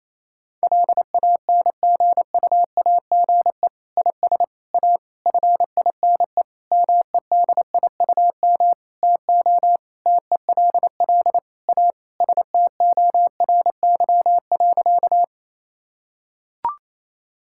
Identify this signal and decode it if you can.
Morse code, heard as a keyed tone, about 28 wpm, 690 Hz. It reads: LANGUAGE IS A FINE MEDIUM TO TELL A STORY.
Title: Sentences from Top 1000 Words - Review Part 6 - 28wpm